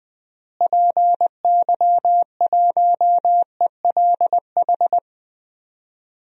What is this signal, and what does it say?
Morse code, heard as a keyed tone, about 20 wpm, 700 Hz. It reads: PY1ELH
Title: DXCC-100 Call Signs - 6 Characters - Repeated 3x - 20wpm